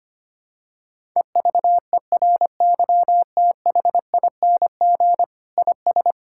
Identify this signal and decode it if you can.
Morse code, heard as a keyed tone, about 25 wpm, 700 Hz. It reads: EVERYTHING IS
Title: Sentences from Top 1000 Words - Review Part 5 - 25wpm